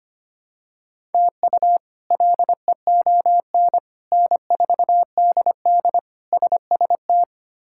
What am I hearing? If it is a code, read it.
Morse code, heard as a keyed tone, about 25 wpm, 700 Hz. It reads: TU LEON N4DD SST
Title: K1USN SST QSO Practice - 25wpm